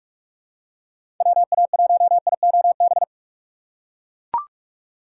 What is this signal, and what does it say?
Morse code, heard as a keyed tone, about 44 wpm, 700 Hz. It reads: WA1IOB
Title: USA Call Signs - 6 Characters - 8x Character Spacing - 45wpm